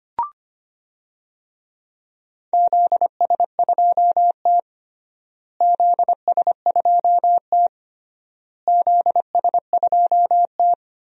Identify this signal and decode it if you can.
Morse code, heard as a keyed tone, about 25 wpm, 700 Hz. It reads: ZS2T ZS2T ZS2T
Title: DXCC-100 Call Signs - 4 Characters - Repeated 3x - 25wpm